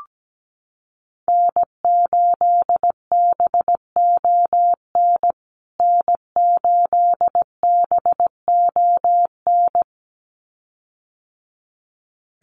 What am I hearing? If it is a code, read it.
Morse code, heard as a keyed tone, about 17 wpm, 700 Hz. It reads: N8BON N8BON
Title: USA Call Signs Repeated 2x 17wpm